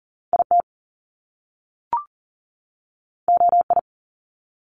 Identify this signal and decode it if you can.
Morse code, heard as a keyed tone, about 38 wpm, 700 Hz. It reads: IT OI
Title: Top Two Letter Combinations in Words 40wpm